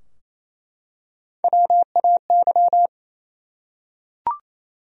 Morse code, heard as a keyed tone, about 28 wpm, 700 Hz. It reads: WAY
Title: Top 100 Words Repeated 28wpm